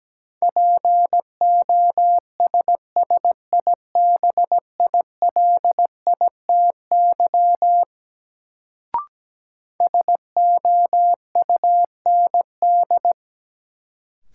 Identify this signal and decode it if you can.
Morse code, heard as a keyed tone, about 17 wpm, 700 Hz. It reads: POSSIBILITY SOUND
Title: Sets of 3 Words - New Word Drill - 17wpm